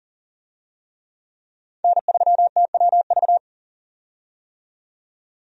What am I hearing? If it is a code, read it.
Morse code, heard as a keyed tone, about 40 wpm, 700 Hz. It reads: N3TWV
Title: USA Call Signs - 5 Characters - 40wpm